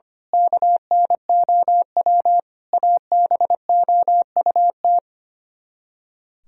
Morse code, heard as a keyed tone, about 24 wpm, 700 Hz. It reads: KNOW ABOUT
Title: Speed-Racing - Sets of 5 Words 25wpm